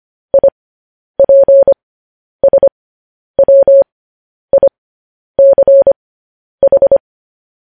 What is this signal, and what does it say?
Morse code, heard as a keyed tone, about 25 wpm, spaced as for 10 wpm, 550 Hz. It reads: IPSWICH
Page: news-headlines-in-morse-code-at-10-wpm